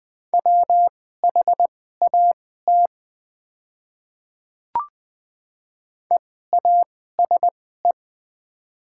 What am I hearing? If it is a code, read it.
Morse code, heard as a keyed tone, about 20 wpm, spaced as for 14 wpm, 700 Hz. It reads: WHAT EASE
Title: 2x Character Spacing - Top 300 Words - 20wpm